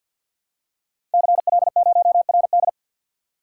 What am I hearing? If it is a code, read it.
Morse code, heard as a keyed tone, about 53 wpm, 700 Hz. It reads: CL0RD
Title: Code Groups - 5 Letter:Numbers - 50wpm